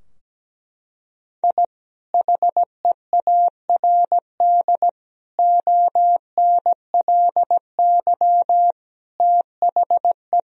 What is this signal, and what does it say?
Morse code, heard as a keyed tone, about 17 wpm, 700 Hz. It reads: I HEARD ONLY THE
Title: Sentences from Top 1000 Words - Review Part 2 - 17wpm